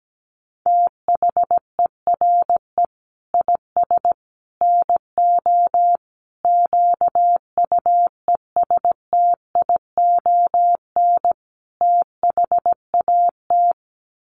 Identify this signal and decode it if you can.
Morse code, heard as a keyed tone, about 17 wpm, 700 Hz. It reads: THERE IS NO QUESTION THAT
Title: Sets of 5 Words Encore 17wpm